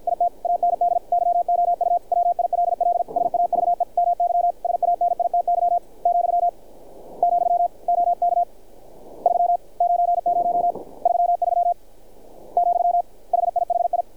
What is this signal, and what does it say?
Morse code, heard as a keyed tone, about 50 wpm, 700 Hz. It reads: IT AND YOU WILL SAVE MY SANITY , , OK 3 90 33 , HIHI
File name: archived / CRSnet